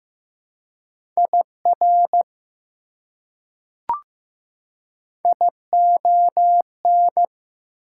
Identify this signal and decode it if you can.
Morse code, heard as a keyed tone, about 15 wpm, 700 Hz. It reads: IR ION
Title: Prefixes and Suffixes 15wpm